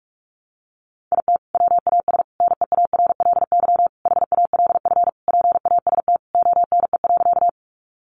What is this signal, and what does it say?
Morse code, heard as a keyed tone, about 44 wpm, 700 Hz. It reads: IT WAS NEARLY HALF PAST ONE.
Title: Sentences from Top 1300 Words 45wpm